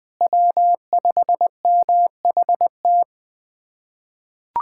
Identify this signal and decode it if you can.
Morse code, heard as a keyed tone, about 20 wpm, 700 Hz. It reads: W5MHT
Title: USA Callsign DE Callsign - 5 Characters - 20wpm